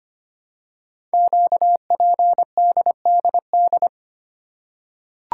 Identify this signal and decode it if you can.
Morse code, heard as a keyed tone, about 25 wpm, 700 Hz. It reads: QPDDD
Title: Code Groups - 5 Letters - 25wpm